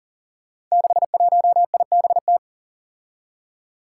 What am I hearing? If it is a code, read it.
Morse code, heard as a keyed tone, about 40 wpm, 700 Hz. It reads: 61IBT
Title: Five Challenging Letter-Number Combinations 40wpm